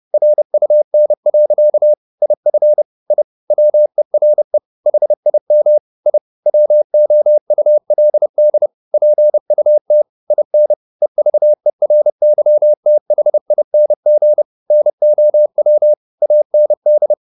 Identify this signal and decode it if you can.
Morse code, heard as a keyed tone, about 30 wpm, 600 Hz. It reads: RUN. IF I WERE HIM I WOULD PUT IN EVERYTHING NOW AND
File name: the old man and the sea / the old man and the sea 1 30wpm